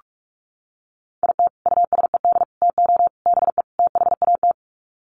Morse code, heard as a keyed tone, about 44 wpm, 700 Hz. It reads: IT USED TO BE THAT
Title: Sets of 5 Words Encore2 45wpm